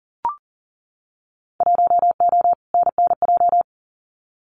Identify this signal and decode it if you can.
Morse code, heard as a keyed tone, about 38 wpm, 700 Hz. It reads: 1O NNJ
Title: Winter Field Day Exchanges 40wpm